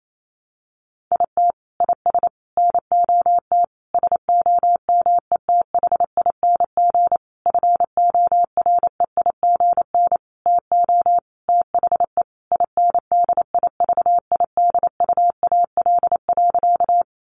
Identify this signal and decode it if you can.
Morse code, heard as a keyed tone, about 28 wpm, 700 Hz. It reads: IT IS NOT SOMETHING FOREIGN TO THE INDIVIDUAL.
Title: Sentences from Top 1400 Words 28wpm